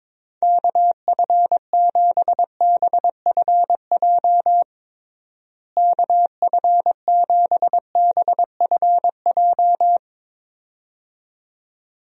Morse code, heard as a keyed tone, about 22 wpm, 700 Hz. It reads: KF7BFJ KF7BFJ
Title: USA Call Signs - 6 Characters - Repeated 2x - 22wpm